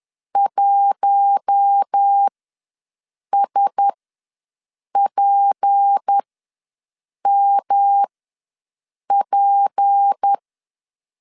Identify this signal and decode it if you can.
Morse code, heard as a keyed tone, about 10 wpm, spaced as for 6 wpm, 800 Hz. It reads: 1SPMP
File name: LEZIONI / 6 PKX0 / 6 P ws 10wpm 3751